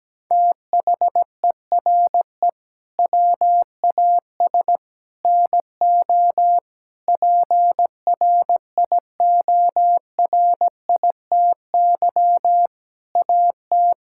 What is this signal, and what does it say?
Morse code, heard as a keyed tone, about 17 wpm, 700 Hz. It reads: THERE WAS NO PRIORITY AT
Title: Sentences from Top 1600 Words 17wpm